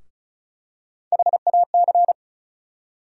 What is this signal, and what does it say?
Morse code, heard as a keyed tone, about 36 wpm, 700 Hz. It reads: HAC
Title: USA Call Sign Postfixes - 3 Letters 35wpm